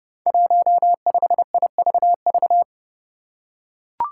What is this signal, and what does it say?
Morse code, heard as a keyed tone, about 31 wpm, 700 Hz. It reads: 15IVV
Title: Five Challenging Letter-Number Combinations 30wpm